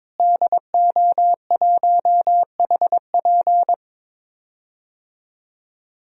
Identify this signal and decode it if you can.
Morse code, heard as a keyed tone, about 22 wpm, 700 Hz. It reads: DO1HP
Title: DXCC-100 Call Signs - 5 Characters - Repeated 2x - 22wpm